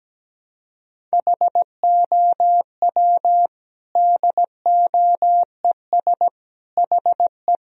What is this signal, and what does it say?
Morse code, heard as a keyed tone, about 17 wpm, 700 Hz. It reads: HOW DOES HE
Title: Sentences from Top 1000 Words - Review Part 2 - 17wpm